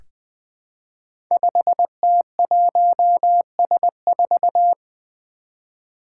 Morse code, heard as a keyed tone, about 20 wpm, 700 Hz. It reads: 5T1S4
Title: Five Challenging Letter-Number Combinations 20wpm